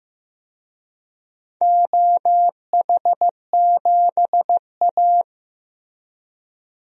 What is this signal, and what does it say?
Morse code, heard as a keyed tone, about 15 wpm, 700 Hz. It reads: OH7A
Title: DXCC-100 Call Signs - 4 Characters - Repeated 2x - 15wpm